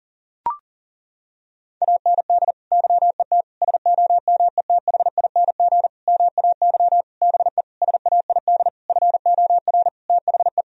Morse code, heard as a keyed tone, about 37 wpm, 700 Hz. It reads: AND YET SOMETHING MAY BE SAID FOR THE
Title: Sentences from Top 1000 Words - Review Part 3 - 40wpm